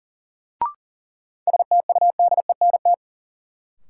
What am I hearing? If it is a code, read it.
Morse code, heard as a keyed tone, about 40 wpm, 700 Hz. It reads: STUDENT
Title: Sentences from Top 1000 Words - New Word Drill - 40wpm